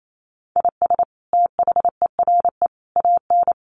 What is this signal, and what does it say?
Morse code, heard as a keyed tone, about 28 wpm, 700 Hz. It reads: IS THERE AN
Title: Sentences from Top 1200 Words 28wpm